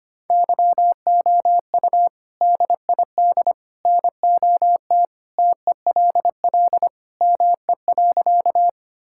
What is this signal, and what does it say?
Morse code, heard as a keyed tone, about 25 wpm, 700 Hz. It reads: YOU DID NOT TELL ME.
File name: practice / 25wpm / Sentences with 3-Word Phrases - 25wpm